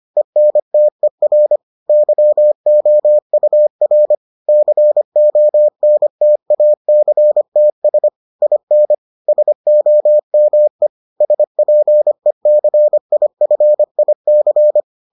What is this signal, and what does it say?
Morse code, heard as a keyed tone, about 25 wpm, 600 Hz. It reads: ENTER YOUR CONTACTS IN SOME SPECIFIC